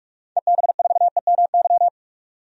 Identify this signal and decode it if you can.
Morse code, heard as a keyed tone, about 46 wpm, 700 Hz. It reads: EB4EMY
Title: DXCC-100 Call Signs - 6 Characters - 45wpm